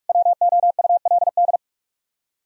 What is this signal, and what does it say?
Morse code, heard as a keyed tone, about 65 wpm, spaced as for 55 wpm, 700 Hz. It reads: WOULD
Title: Speed-Racing - Sets of 2 Words 45wpm